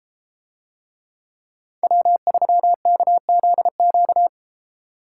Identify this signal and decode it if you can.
Morse code, heard as a keyed tone, about 33 wpm, 700 Hz. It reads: W3KZQ